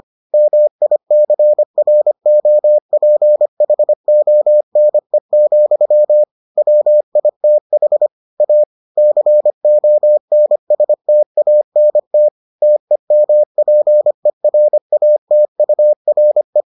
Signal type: Morse code

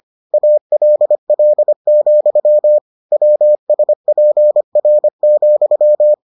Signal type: Morse code

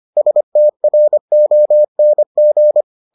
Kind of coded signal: Morse code

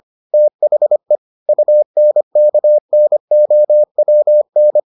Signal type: Morse code

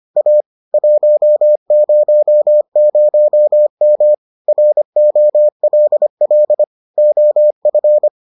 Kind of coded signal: Morse code

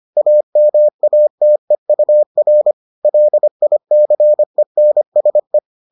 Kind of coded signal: Morse code

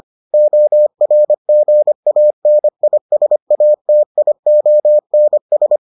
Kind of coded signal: Morse code

